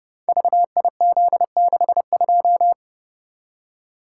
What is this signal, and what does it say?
Morse code, heard as a keyed tone, about 30 wpm, 700 Hz. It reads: VIZ62
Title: Code Groups - 5 Letter:Numbers - 30wpm